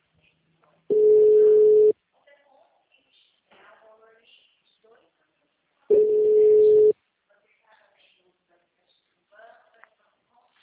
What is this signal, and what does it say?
Ringback tone (Europe, 425 Hz), with speech.